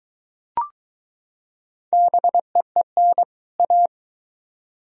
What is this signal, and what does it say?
Morse code, heard as a keyed tone, about 21 wpm, 700 Hz. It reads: BEEN A